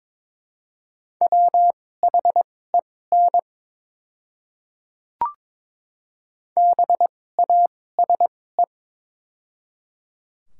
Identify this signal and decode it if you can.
Morse code, heard as a keyed tone, about 21 wpm, spaced as for 16 wpm, 700 Hz. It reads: WHEN BASE
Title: 2x Character Spacing - Top 500 Words - 22wpm